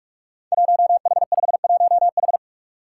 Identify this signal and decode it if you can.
Morse code, heard as a keyed tone, about 44 wpm, 700 Hz. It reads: 1H51H
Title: Five Challenging Letter-Number Combinations 45wpm